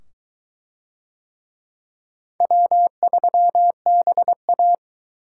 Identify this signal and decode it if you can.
Morse code, heard as a keyed tone, about 22 wpm, 700 Hz. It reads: W3BA